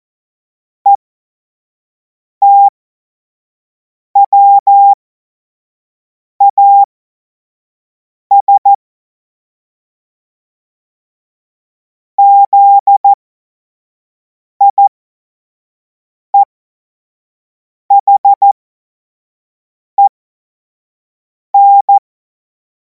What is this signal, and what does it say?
Morse code, heard as a keyed tone, about 14 wpm, spaced as for 5 wpm, 800 Hz. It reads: ETWAS ZIEHEN
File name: Morsekurs / Witze / Witz 10. März 2026 25 F70